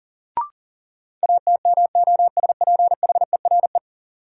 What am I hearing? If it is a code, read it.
Morse code, heard as a keyed tone, about 38 wpm, 700 Hz. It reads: ATMOSPHERE
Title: Sentences from Top 1600 Words - New Word Drill - 40wpm